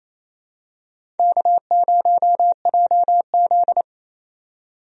Morse code, heard as a keyed tone, about 28 wpm, 700 Hz. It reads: K0JZ
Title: USA Call Signs - 4 Characters - 28wpm